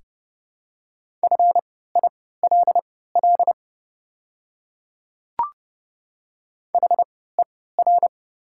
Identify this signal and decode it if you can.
Morse code, heard as a keyed tone, about 30 wpm, spaced as for 17 wpm, 700 Hz. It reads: FILL HER
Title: 3x Character Spacing - Top 500 Words - 30wpm